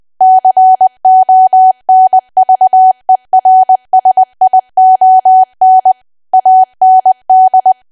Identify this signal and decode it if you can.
Morse code, heard as a keyed tone, about 20 wpm, 750 Hz. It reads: CONVERSION AND
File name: cwa / W1AW-Level3 / 20WPM